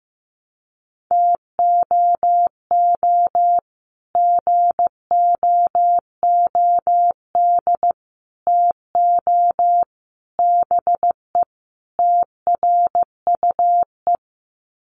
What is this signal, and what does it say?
Morse code, heard as a keyed tone, about 15 wpm, 700 Hz. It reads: TOO GOOD TO BE TRUE
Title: Sets of 5 Words Encore 15wpm